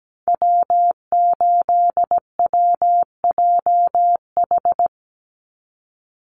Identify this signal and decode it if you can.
Morse code, heard as a keyed tone, about 17 wpm, 700 Hz. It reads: W8WJH
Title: USA Call Signs 17wpm